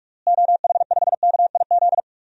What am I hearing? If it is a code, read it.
Morse code, heard as a keyed tone, about 44 wpm, 700 Hz. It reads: OH5KIZ
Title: DXCC-100 Call Signs - 6 Characters - Repeated 3x - 45wpm